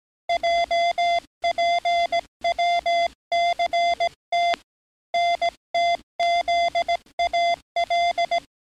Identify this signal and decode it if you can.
Morse code, heard as a keyed tone, about 18 wpm, 690 Hz. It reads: JPWCT NTZAL
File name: training / NAVPERS 11049to11060 / NAVPERS 11057 RA RB